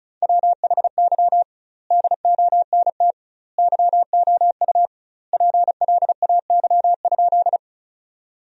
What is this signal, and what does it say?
Morse code, heard as a keyed tone, about 34 wpm, 700 Hz. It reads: WHY DONT YOU PLAY?
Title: 2x Word-Spacing - Sentences from Top 300 Words - 35wpm